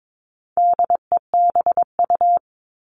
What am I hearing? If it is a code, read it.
Morse code, heard as a keyed tone, about 22 wpm, 700 Hz. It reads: DEBU
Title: All Cut Numbers - 4-Digit 22wpm